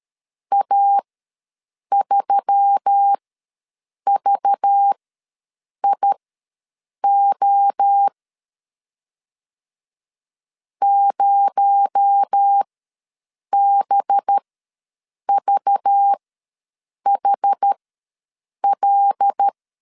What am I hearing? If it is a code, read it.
Morse code, heard as a keyed tone, about 13 wpm, spaced as for 7 wpm, 800 Hz. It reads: A3VIO 0BVHL
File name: LEZIONI / 6 PKX0 / 6 PKX0 ws 12wpm 3448